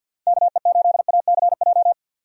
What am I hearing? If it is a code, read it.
Morse code, heard as a keyed tone, about 48 wpm, 700 Hz. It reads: KE8ACJ
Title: USA Call Signs - 6 Characters - 50wpm